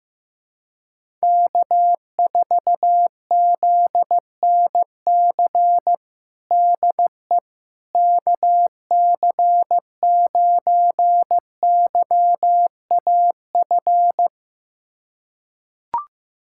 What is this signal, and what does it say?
Morse code, heard as a keyed tone, about 15 wpm, 700 Hz. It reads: K4ZNC DE KC9YAF
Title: USA Callsign DE Callsign 15wpm